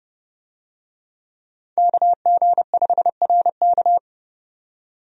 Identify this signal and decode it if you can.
Morse code, heard as a keyed tone, about 32 wpm, 700 Hz. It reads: KG5RK
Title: USA Call Signs - 5 Characters - 30wpm